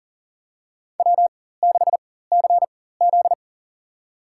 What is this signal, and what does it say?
Morse code, heard as a keyed tone, about 40 wpm, spaced as for 19 wpm, 700 Hz. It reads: W6CZ
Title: USA Call Signs - 4 Characters - 4x Character Spacing - 40wpm